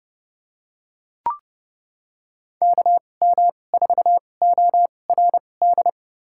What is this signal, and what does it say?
Morse code, heard as a keyed tone, about 30 wpm, spaced as for 22 wpm, 700 Hz. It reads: KM4ORD
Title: USA Call Signs - 6 Characters - 2x Character Spacing - 30wpm